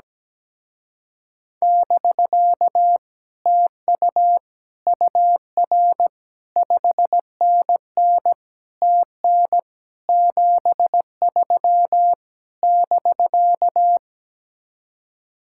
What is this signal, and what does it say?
Morse code, heard as a keyed tone, about 17 wpm, 700 Hz. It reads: <BK> TU UR 5NN TN 73 <BK>
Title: Speed-Racing Common QSO Phrases Encore - 17wpm